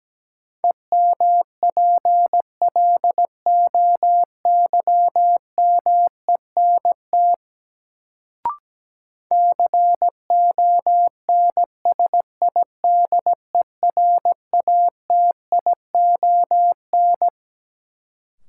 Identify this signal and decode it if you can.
Morse code, heard as a keyed tone, about 17 wpm, 700 Hz. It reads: EMPLOYMENT CONSIDERATION
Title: Sentences from Top 1500 Words - New Word Drill - 17wpm